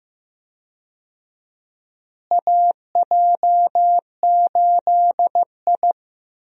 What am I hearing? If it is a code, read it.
Morse code, heard as a keyed tone, about 15 wpm, 700 Hz. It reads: AJ8I
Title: USA Call Signs - 4 Characters - Repeated 2x - 15wpm